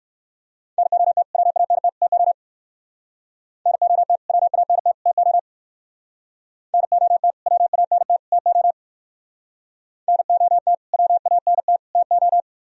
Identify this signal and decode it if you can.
Morse code, heard as a keyed tone, about 57 wpm, 700 Hz. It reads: NOT WANT TO NOT WANT TO NOT WANT TO NOT WANT TO